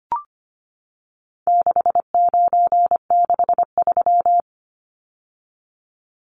Morse code, heard as a keyed tone, about 25 wpm, 700 Hz. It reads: 6963
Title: Numbers - 4-Digit 25wpm